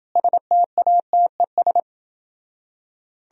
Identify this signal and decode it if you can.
Morse code, heard as a keyed tone, about 29 wpm, 700 Hz. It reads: STATES